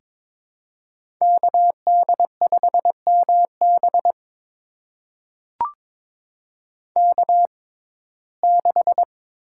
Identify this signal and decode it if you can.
Morse code, heard as a keyed tone, about 22 wpm, spaced as for 16 wpm, 700 Hz. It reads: KD5MB K6
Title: USA Call Signs - 5 Characters - 6x Character Spacing - 22wpm